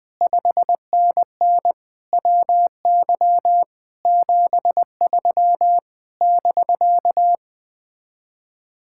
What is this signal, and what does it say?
Morse code, heard as a keyed tone, about 20 wpm, 700 Hz. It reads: 5NN WY 73 <BK>
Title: Speed-Racing Common QSO Phrases - 17wpm